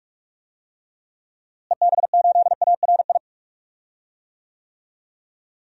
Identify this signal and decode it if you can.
Morse code, heard as a keyed tone, about 42 wpm, 700 Hz. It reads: EB8ARI
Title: DXCC-100 Call Signs - 6 Characters - Repeated 3x - 45wpm